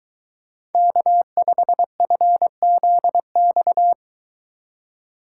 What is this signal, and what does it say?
Morse code, heard as a keyed tone, about 23 wpm, 700 Hz. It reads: K5FZX